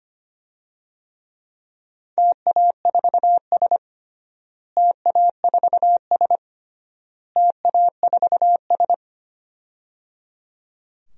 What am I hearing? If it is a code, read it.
Morse code, heard as a keyed tone, about 25 wpm, 700 Hz. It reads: TA4S TA4S TA4S
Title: DXCC-100 Call Signs - 4 Characters - Repeated 3x - 25wpm